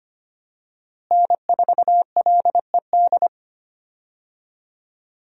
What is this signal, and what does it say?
Morse code, heard as a keyed tone, about 25 wpm, 700 Hz. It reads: N4LED
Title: USA Call Signs - 5 Characters - Repeated 2x - 25wpm